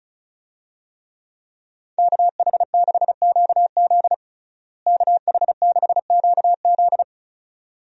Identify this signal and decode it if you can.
Morse code, heard as a keyed tone, about 36 wpm, 700 Hz. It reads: KH6QZ KH6QZ
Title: USA Call Signs - 5 Characters - Repeated 2x - 35wpm